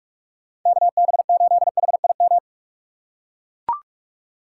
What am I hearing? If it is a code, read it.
Morse code, heard as a keyed tone, about 44 wpm, 700 Hz. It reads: KB8HIM
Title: USA Call Signs - 6 Characters - 2x Character Spacing - 45wpm